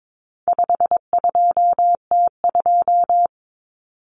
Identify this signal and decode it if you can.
Morse code, heard as a keyed tone, about 22 wpm, 700 Hz. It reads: 52T2
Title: Common Cut Numbers - 4-Digit 22wpm